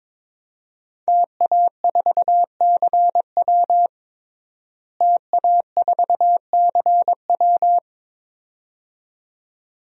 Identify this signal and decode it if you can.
Morse code, heard as a keyed tone, about 22 wpm, 700 Hz. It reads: TA4CW TA4CW
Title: DXCC-100 Call Signs - 5 Characters - Repeated 2x - 22wpm